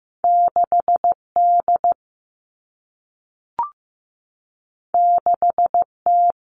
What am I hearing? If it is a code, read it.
Morse code, heard as a keyed tone, about 15 wpm, 700 Hz. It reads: 6D 6T
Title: All Cut Numbers - 2-Digit 15wpm